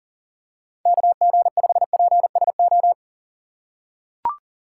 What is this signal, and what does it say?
Morse code, heard as a keyed tone, about 40 wpm, 700 Hz. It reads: KG5PSO
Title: USA Call Signs - 6 Characters - 4x Character Spacing - 40wpm